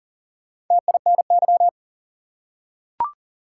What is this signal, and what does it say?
Morse code, heard as a keyed tone, about 40 wpm, 700 Hz. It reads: TINY
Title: Four Letter Words - 40wpm